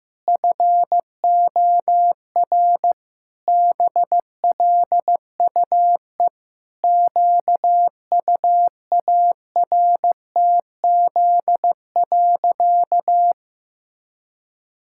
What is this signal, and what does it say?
Morse code, heard as a keyed tone, about 16 wpm, 700 Hz. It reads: FOR BLUE QUARTZ.
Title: Pangram Challenge v2 - 15wpm